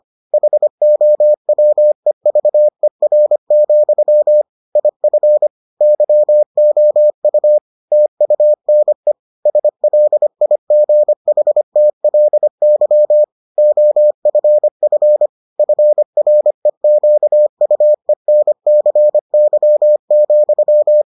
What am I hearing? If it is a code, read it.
Morse code, heard as a keyed tone, about 25 wpm, 600 Hz. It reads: HOWEVER, IF YOU TUNE SLIGHTLY OFF FREQUENCY,